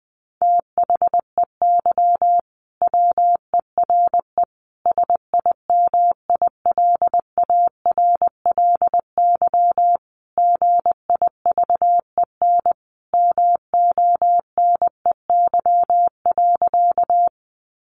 Morse code, heard as a keyed tone, about 20 wpm, 700 Hz. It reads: THEY WERE SIMILARLY GIVEN MONEY.
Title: Sentences from Top 1400 Words 20wpm